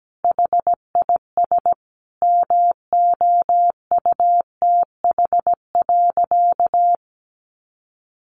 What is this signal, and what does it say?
Morse code, heard as a keyed tone, about 17 wpm, 700 Hz. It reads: HIS MOUTH.
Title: Sentences from Top 1500 Words 17wpm